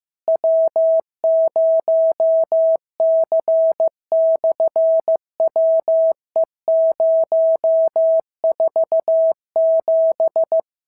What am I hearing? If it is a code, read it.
Morse code, heard as a keyed tone, about 15 wpm, 650 Hz. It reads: W0C/WE047
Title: SOTA QSO Practice 15wpm